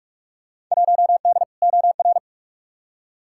Morse code, heard as a keyed tone, about 47 wpm, 700 Hz. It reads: 1D OR
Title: Field Day Exchanges - 45wpm